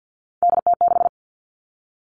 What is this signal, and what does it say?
Morse code, heard as a keyed tone, about 50 wpm, 700 Hz. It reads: DT6
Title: All Cut Numbers - 3-Digit 50wpm